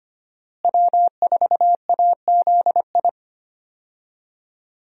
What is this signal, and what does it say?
Morse code, heard as a keyed tone, about 25 wpm, 700 Hz. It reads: W4AZI
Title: USA Call Signs - 5 Characters - Repeated 3x - 25wpm